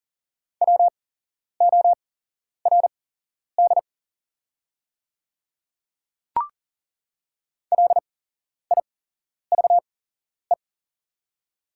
Morse code, heard as a keyed tone, about 40 wpm, spaced as for 11 wpm, 690 Hz. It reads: WORD LIVE
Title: 8x Character Spacing - Top 200 Words - 40wpm